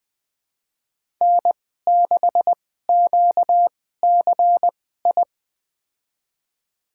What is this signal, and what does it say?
Morse code, heard as a keyed tone, about 20 wpm, spaced as for 14 wpm, 700 Hz. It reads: N6QCI
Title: USA Call Signs - 5 Characters - 2x Character Spacing - 20wpm